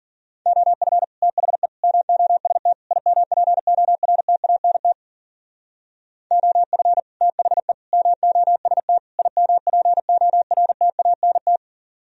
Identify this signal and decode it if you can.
Morse code, heard as a keyed tone, about 47 wpm, 700 Hz. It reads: OF THE MOST IMPORTANT OF THE MOST IMPORTANT
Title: Speed-Racing - Sets of 4 Words 40wpm